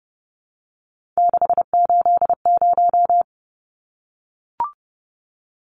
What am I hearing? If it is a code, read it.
Morse code, heard as a keyed tone, about 30 wpm, 700 Hz. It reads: 680
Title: Numbers - 3-Digit 30wpm